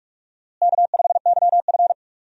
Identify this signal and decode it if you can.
Morse code, heard as a keyed tone, about 42 wpm, 700 Hz. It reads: K5YF
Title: USA Call Signs - 4 Characters - Repeated 2x - 45wpm